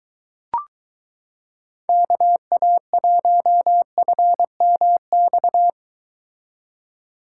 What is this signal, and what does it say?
Morse code, heard as a keyed tone, about 22 wpm, 700 Hz. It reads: KA1FMX